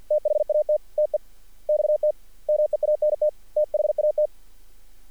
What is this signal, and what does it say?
Morse code, heard as a keyed tone, about 48 wpm, spaced as for 41 wpm, 610 Hz. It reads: THAT TEXT MEANT THAT
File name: archived / CRSnet